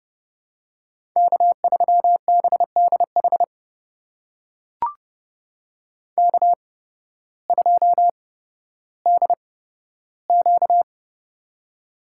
Morse code, heard as a keyed tone, about 30 wpm, spaced as for 14 wpm, 700 Hz. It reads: K3BDH K2DQ
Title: USA Call Signs - 8x Character Spacing - 30wpm